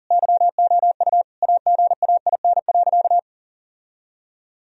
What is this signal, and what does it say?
Morse code, heard as a keyed tone, about 40 wpm, 700 Hz. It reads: YOU AGAIN.
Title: Sentences from Top 1000 Words - Review Part 1 - 40wpm